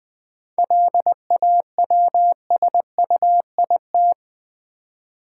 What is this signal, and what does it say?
Morse code, heard as a keyed tone, about 20 wpm, 700 Hz. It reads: LAWSUIT
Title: 7 Letter Words - First 3 Letters Given - 20wpm